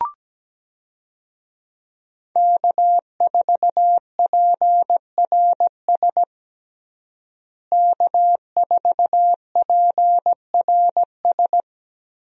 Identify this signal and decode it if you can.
Morse code, heard as a keyed tone, about 17 wpm, 700 Hz. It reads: K4PRS K4PRS
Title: USA Call Signs - 5 Characters - Repeated 3x - 17wpm